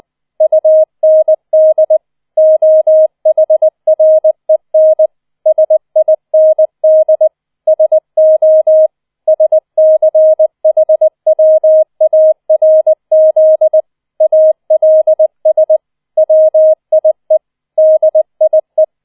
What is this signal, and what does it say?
Morse code, heard as a keyed tone, about 20 wpm, 630 Hz. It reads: UND OHREN SIND SO SCHWARZ ALS WIE DIE